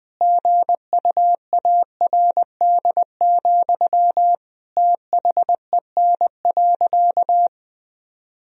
Morse code, heard as a keyed tone, about 20 wpm, 700 Hz. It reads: GUARD, THEN.
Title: Sentences from Top 1000 Words - Review Part 4 - 20wpm